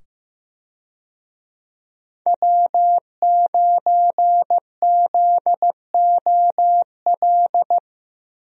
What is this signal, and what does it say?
Morse code, heard as a keyed tone, about 15 wpm, 700 Hz. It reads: W9ZOL
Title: USA Call Signs - 5 Characters - 15wpm